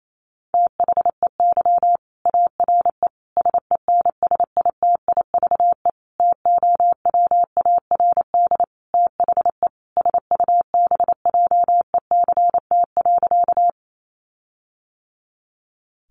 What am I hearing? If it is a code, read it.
Morse code, heard as a keyed tone, about 28 wpm, 700 Hz. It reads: THEY ARE SENSITIVE TOWARD THE SUBJECT.
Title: Sentences from Top 1400 Words 28wpm